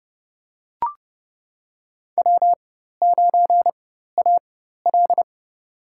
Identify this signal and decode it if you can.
Morse code, heard as a keyed tone, about 30 wpm, spaced as for 14 wpm, 700 Hz. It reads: W9AL
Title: USA Call Signs - 4x Character Spacing - 30wpm